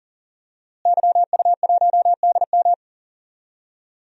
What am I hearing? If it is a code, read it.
Morse code, heard as a keyed tone, about 40 wpm, 700 Hz. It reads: YU1DM
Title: DXCC-100 Call Signs - 5 Characters - Repeated 3x - 40wpm